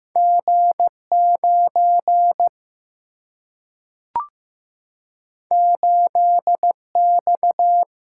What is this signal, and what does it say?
Morse code, heard as a keyed tone, about 15 wpm, 700 Hz. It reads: G 9 8 X
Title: Single-Letter Single-Number 15wpm